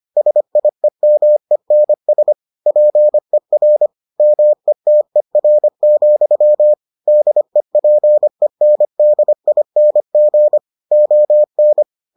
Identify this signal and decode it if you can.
Morse code, heard as a keyed tone, about 25 wpm, 600 Hz. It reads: SIEMENS PER METER, DEPENDING ON